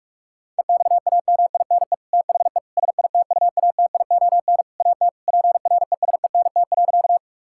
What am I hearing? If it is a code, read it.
Morse code, heard as a keyed tone, about 46 wpm, 700 Hz. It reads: EXAMINE THE SITUATION AT PRESENT.
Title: Sentences from Top 1600 Words 45wpm